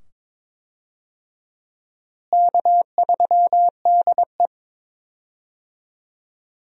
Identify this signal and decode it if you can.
Morse code, heard as a keyed tone, about 22 wpm, 700 Hz. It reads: K3DE
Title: USA Call Signs - 4 Characters - Repeated 2x - 22wpm